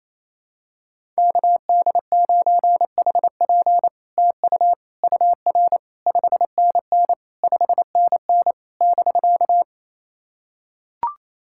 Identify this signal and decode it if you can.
Morse code, heard as a keyed tone, about 27 wpm, 700 Hz. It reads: KD9HP TU UR 5NN 5NN <BK>
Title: SOTA QSO Practice 28wpm